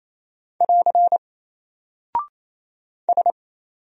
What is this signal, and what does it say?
Morse code, heard as a keyed tone, about 28 wpm, 700 Hz. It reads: <AR> S
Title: Single Character (Letter:Number:Punctuation:Prosign) Alt 28wpm